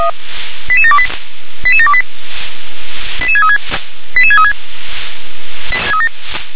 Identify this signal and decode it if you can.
Five-tone selective call (ZVEI), heard as quick runs of five tones, tones of about 70 ms. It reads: 88728 88728 88636 88636
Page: /357/SDR/ZVEI/